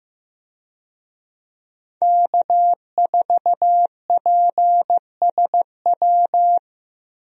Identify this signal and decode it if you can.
Morse code, heard as a keyed tone, about 15 wpm, 700 Hz. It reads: K4PSW
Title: USA Call Signs - 5 Characters - 15wpm